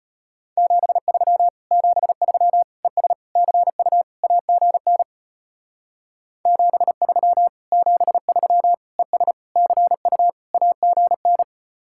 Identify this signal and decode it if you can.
Morse code, heard as a keyed tone, about 30 wpm, 700 Hz. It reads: 73 73 ES CU AGN 73 73 ES CU AGN
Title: Speed-Racing Common QSO Phrases - 25wpm